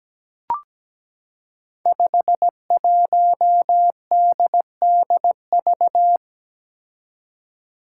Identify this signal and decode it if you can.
Morse code, heard as a keyed tone, about 17 wpm, 700 Hz. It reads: 51DDV
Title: Five Challenging Letter-Number Combinations 17wpm